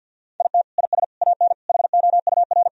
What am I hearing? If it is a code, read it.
Morse code, heard as a keyed tone, about 50 wpm, 700 Hz. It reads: IT IS AN HOUR
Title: Sentences from Top 1000 Words - Review Part 2 - 50wpm